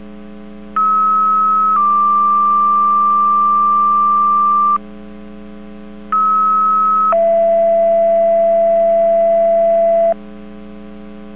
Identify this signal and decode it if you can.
Two-tone page, heard as a one-second tone, then a three-second tone, about 1290 then 1180 Hz, about 1290 then 670 Hz. It is heard from 0.8 to 10.1 s.